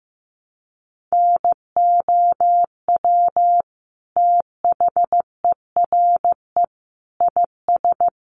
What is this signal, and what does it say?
Morse code, heard as a keyed tone, about 14 wpm, 700 Hz. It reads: NOW THERE IS
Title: Sentences from Top 100 Words 15wpm